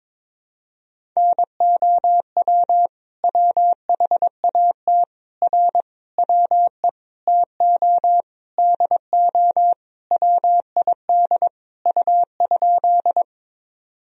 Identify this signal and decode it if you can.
Morse code, heard as a keyed tone, about 22 wpm, 700 Hz. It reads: NOW WHAT R WE TO DO WID U?
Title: Sentences from Top 100 Words and Abbreviations 22wpm